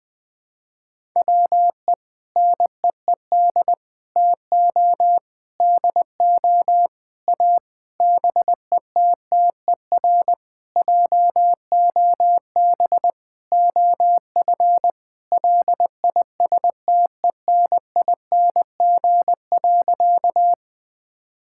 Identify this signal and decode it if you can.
Morse code, heard as a keyed tone, about 20 wpm, 700 Hz. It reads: WE NEED TO DO A BETTER JOB OF LISTENING.
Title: Sentences with 6-Word Phrases - 20wpm